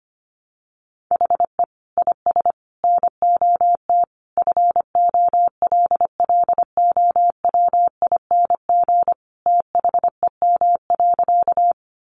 Morse code, heard as a keyed tone, about 25 wpm, 700 Hz. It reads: HE IS NOT FOLLOWING THEM.
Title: Sentences from Top 700 Words 25wpm